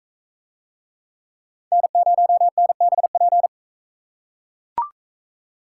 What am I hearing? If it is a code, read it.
Morse code, heard as a keyed tone, about 40 wpm, 700 Hz. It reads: N0NBP